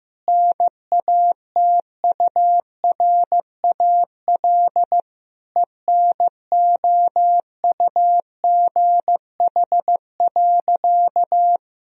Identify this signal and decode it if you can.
Morse code, heard as a keyed tone, about 15 wpm, 700 Hz. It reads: NATURAL ENOUGH.
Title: Sentences from Top 1000 Words - Review Part 4 - 15wpm